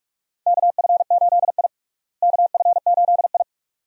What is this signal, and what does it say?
Morse code, heard as a keyed tone, about 46 wpm, 700 Hz. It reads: KF8I KF8I
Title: USA Call Signs - 4 Characters - Repeated 2x - 45wpm